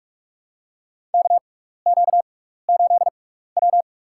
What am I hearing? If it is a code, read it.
Morse code, heard as a keyed tone, about 44 wpm, spaced as for 15 wpm, 700 Hz. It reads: KQ8W
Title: USA Call Signs - 4 Characters - 6x Character Spacing - 45wpm